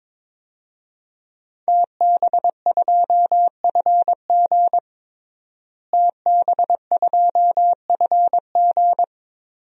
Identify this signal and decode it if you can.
Morse code, heard as a keyed tone, about 22 wpm, 700 Hz. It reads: TB2FG TB2FG
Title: DXCC-100 Call Signs - 5 Characters - Repeated 3x - 22wpm